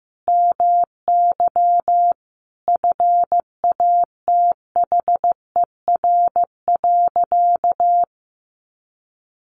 Morse code, heard as a keyed tone, about 15 wpm, 700 Hz. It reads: MY FATHER.
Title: Sentences from Top 1400 Words 15wpm